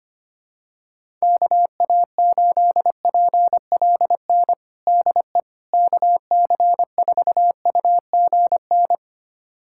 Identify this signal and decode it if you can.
Morse code, heard as a keyed tone, about 25 wpm, 700 Hz. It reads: KA8PLN DE KC4UGN
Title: USA Callsign DE Callsign 25wpm